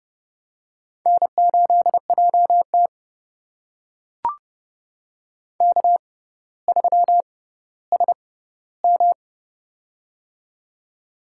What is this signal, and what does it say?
Morse code, heard as a keyed tone, about 30 wpm, spaced as for 15 wpm, 700 Hz. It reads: N8JT K3SM
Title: USA Call Signs - 4 Characters - 6x Character Spacing - 30wpm